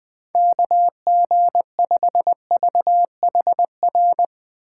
Morse code, heard as a keyed tone, about 20 wpm, 700 Hz. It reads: KG5VHR
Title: USA Call Signs - 6 Characters - 4x Character Spacing - 20wpm